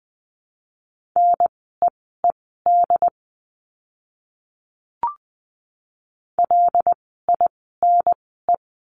Morse code, heard as a keyed tone, about 20 wpm, spaced as for 14 wpm, 700 Hz. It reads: NEED LINE
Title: 115 Lesson 19f - Letter w - Words 20wpm